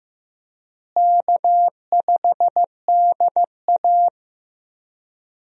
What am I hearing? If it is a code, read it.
Morse code, heard as a keyed tone, about 15 wpm, 700 Hz. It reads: K5DA
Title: USA Call Signs - 4 Characters - Repeated 2x - 15wpm